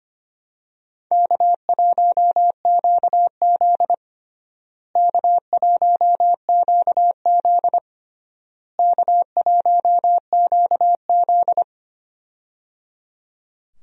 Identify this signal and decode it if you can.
Morse code, heard as a keyed tone, about 25 wpm, 700 Hz. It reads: K1QZ K1QZ K1QZ
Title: USA Call Signs - 4 Characters - Repeated 3x - 25wpm